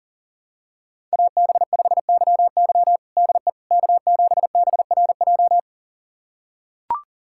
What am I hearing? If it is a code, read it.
Morse code, heard as a keyed tone, about 40 wpm, 700 Hz. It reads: AB5YY DE K7BRJ
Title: USA Callsign DE Callsign - 5 Characters - 40wpm